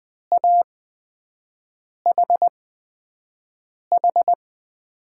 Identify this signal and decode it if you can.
Morse code, heard as a keyed tone, about 20 wpm, spaced as for 5 wpm, 700 Hz. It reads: AHH
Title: USA Call Signs - 5 Characters - 8x Character Spacing - 20wpm